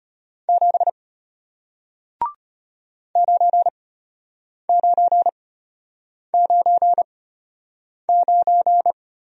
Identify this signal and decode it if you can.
Morse code, heard as a keyed tone, about 33 wpm, spaced as for 8 wpm, 700 Hz. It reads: 7 9999